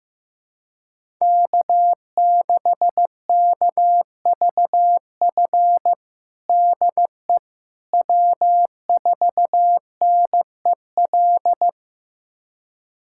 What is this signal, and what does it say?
Morse code, heard as a keyed tone, about 15 wpm, 700 Hz. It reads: K6KVF DE W4NEL
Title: USA Callsign DE Callsign - 5 Characters - 15wpm